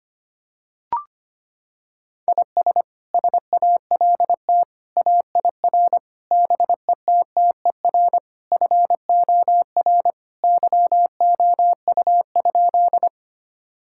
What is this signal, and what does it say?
Morse code, heard as a keyed tone, about 25 wpm, 700 Hz. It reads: IS SALT AIR BETTER FOR YOU?
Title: Sentences from Top 1000 Words - Review Part 5 - 25wpm